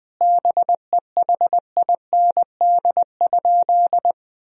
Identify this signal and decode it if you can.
Morse code, heard as a keyed tone, about 21 wpm, 700 Hz. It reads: BEHIND?
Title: Sentences from Top 1000 Words - Review Part 2 - 20wpm